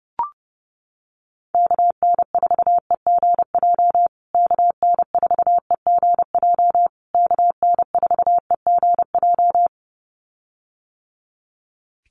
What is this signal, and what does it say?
Morse code, heard as a keyed tone, about 30 wpm, 700 Hz. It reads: KN4EGJ KN4EGJ KN4EGJ
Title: USA Call Signs Repeated 3x 30wpm